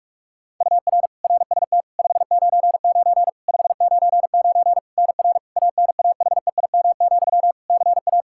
Morse code, heard as a keyed tone, about 46 wpm, 700 Hz. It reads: UR RST 599 599 NR ANAHEIM, CA
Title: Practice QSO - Exchange 1 - Beginning - 45wpm